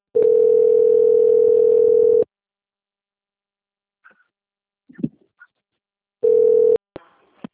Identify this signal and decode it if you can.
Ringback tone (North America), 440 + 480 Hz.
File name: out- - - - - .25414